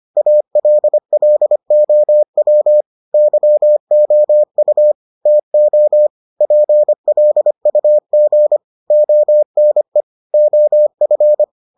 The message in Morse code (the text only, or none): ALLOW YOU TO PLUG ONE OF